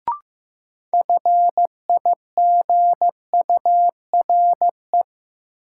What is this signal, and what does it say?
Morse code, heard as a keyed tone, about 15 wpm, 700 Hz. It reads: FIGURE
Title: Sentences from Top 1000 Words - Review Part 2 - New Word Drill - 15wpm